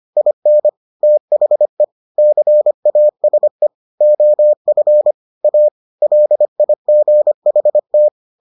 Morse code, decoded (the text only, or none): IN THE CASE OF A LIGHT